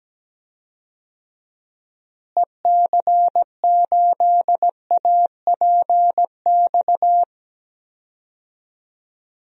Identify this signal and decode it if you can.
Morse code, heard as a keyed tone, about 17 wpm, 700 Hz. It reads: EC8APX
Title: DXCC-100 Call Signs - 6 Characters - Repeated 3x - 17wpm